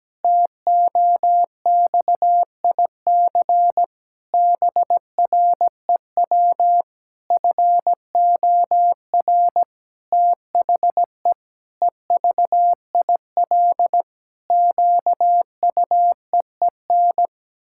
Morse code, heard as a keyed tone, about 17 wpm, 700 Hz. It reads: TOXIC BREW FOR THE EVIL QUEEN
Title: Pangram Challenge v2 - 17wpm